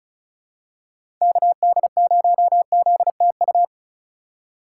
Morse code, heard as a keyed tone, about 34 wpm, 700 Hz. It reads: KD0ZTU
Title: USA Call Signs - 6 Characters - 2x Character Spacing - 35wpm